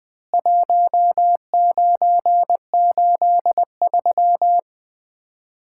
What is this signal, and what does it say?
Morse code, heard as a keyed tone, about 20 wpm, 700 Hz. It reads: 1983
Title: Common QSO Phrases - 2x Word-Spacing - 20wpm